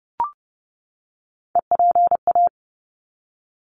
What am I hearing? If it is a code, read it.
Morse code, heard as a keyed tone, about 30 wpm, 700 Hz. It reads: EPA
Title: ARRL Sections 30wpm